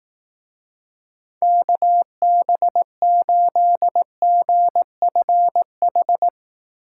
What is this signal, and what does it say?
Morse code, heard as a keyed tone, about 18 wpm, 700 Hz. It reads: KB8GFH
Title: Speed Racing - USA 2x3 Call Signs 15wpm